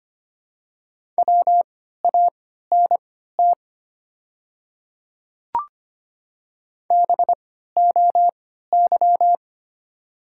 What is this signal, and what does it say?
Morse code, heard as a keyed tone, about 25 wpm, spaced as for 14 wpm, 690 Hz. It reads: WANT BOY
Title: 3x Character Spacing - Top 300 Words - 25wpm